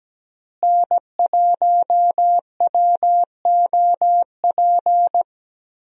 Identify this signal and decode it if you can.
Morse code, heard as a keyed tone, about 17 wpm, 700 Hz. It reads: N1WOP
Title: USA Call Signs - 5 Characters - Repeated 3x - 17wpm